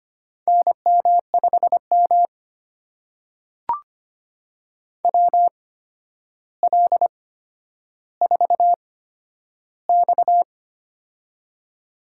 Morse code, heard as a keyed tone, about 25 wpm, spaced as for 11 wpm, 700 Hz. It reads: NM5M WL4X
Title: USA Call Signs - 4 Characters - 8x Character Spacing - 25wpm